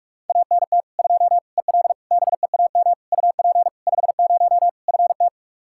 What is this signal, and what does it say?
Morse code, heard as a keyed tone, about 46 wpm, 700 Hz. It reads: ANT 2 EL BEAM UP 50 FT
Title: Practice QSO - Exchange 2 - Info - 45wpm